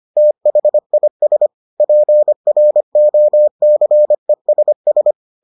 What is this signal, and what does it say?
Morse code, heard as a keyed tone, about 25 wpm, 600 Hz. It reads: THIS PROCESS